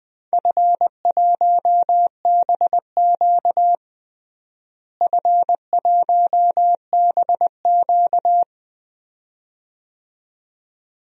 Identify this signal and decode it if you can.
Morse code, heard as a keyed tone, about 20 wpm, 700 Hz. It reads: F1BQ F1BQ
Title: DXCC-100 Call Signs - 4 Characters - Repeated 2x - 20wpm